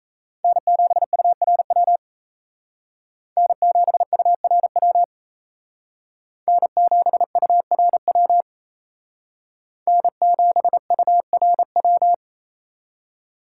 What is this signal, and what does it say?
Morse code, heard as a keyed tone, about 32 wpm, 700 Hz. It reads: N7URW N7URW N7URW N7URW
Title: Speed Racing - USA 1x3 Call Signs 28wpm